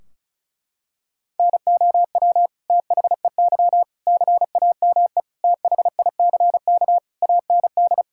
Morse code, heard as a keyed tone, about 36 wpm, 700 Hz. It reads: NOW THEY CAME THICK AND
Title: Sentences from Top 1600 Words 35wpm